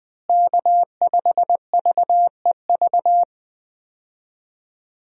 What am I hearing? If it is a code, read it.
Morse code, heard as a keyed tone, about 20 wpm, 700 Hz. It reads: K5VEV
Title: USA Call Signs - 5 Characters - Repeated 3x - 20wpm